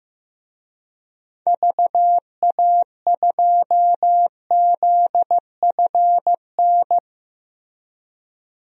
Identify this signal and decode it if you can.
Morse code, heard as a keyed tone, about 15 wpm, 700 Hz. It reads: VA2ZFN
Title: DXCC-100 Call Signs - 6 Characters - Repeated 3x - 15wpm